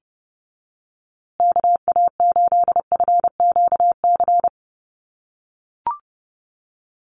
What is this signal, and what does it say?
Morse code, heard as a keyed tone, about 30 wpm, 700 Hz. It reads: KA8FQC
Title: USA Call Signs Repeated 3x 30wpm